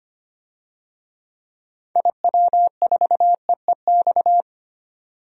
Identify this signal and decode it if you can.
Morse code, heard as a keyed tone, about 25 wpm, 700 Hz. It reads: IW4EEX
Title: DXCC-100 Call Signs - 6 Characters - 25wpm